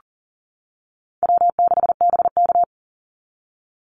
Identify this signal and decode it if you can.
Morse code, heard as a keyed tone, about 40 wpm, 700 Hz. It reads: W6BK
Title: Easy Call Signs - 40wpm